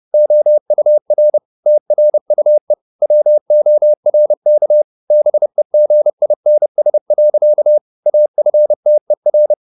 Morse code, heard as a keyed tone, about 30 wpm, 600 Hz. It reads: OUR TRUE WORK BEGINS. AFTER